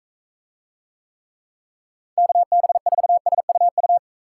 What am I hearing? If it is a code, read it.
Morse code, heard as a keyed tone, about 42 wpm, 700 Hz. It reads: KB4SUU